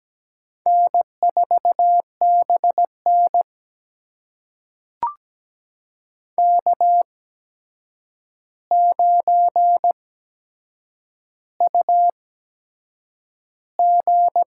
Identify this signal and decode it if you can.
Morse code, heard as a keyed tone, about 17 wpm, spaced as for 7 wpm, 700 Hz. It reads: N4BN K9UG
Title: USA Call Signs - 4 Characters - 8x Character Spacing - 17wpm